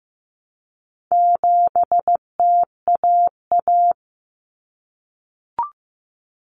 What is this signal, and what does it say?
Morse code, heard as a keyed tone, about 15 wpm, 700 Hz. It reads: 7TAA
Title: Common Cut Numbers - 4-Digit 15wpm